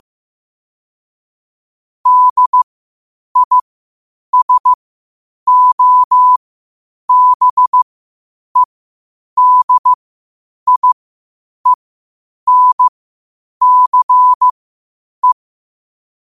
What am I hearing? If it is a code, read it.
Morse code, heard as a keyed tone, about 15 wpm, spaced as for 8 wpm, 1000 Hz. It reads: DISOBEDIENCE
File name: qotdfiles / 8wpm